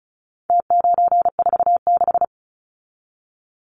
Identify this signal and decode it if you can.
Morse code, heard as a keyed tone, about 34 wpm, 700 Hz. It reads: T946
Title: Numbers - Contest Style - 4-Digit 35wpm